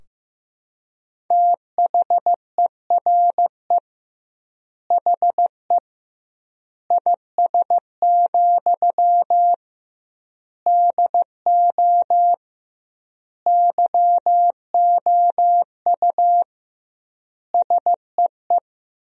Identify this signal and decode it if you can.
Morse code, heard as a keyed tone, about 15 wpm, 700 Hz. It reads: THERE HE IS, DO YOU SEE
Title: 2x Word-Spacing - Sentences from Top 100 Words - 15wpm